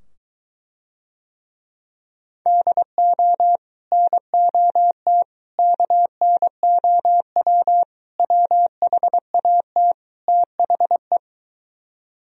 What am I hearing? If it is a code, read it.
Morse code, heard as a keyed tone, about 23 wpm, 700 Hz. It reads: DO NOT KNOW WHAT THE